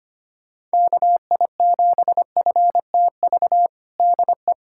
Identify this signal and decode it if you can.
Morse code, heard as a keyed tone, about 25 wpm, 700 Hz. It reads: KI7FTV DE
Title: USA Callsign DE Callsign - 6 Characters - 25wpm